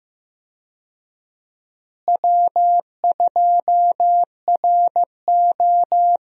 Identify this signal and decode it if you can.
Morse code, heard as a keyed tone, about 15 wpm, 700 Hz. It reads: W2RO
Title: USA Call Signs - 4 Characters - Repeated 3x - 15wpm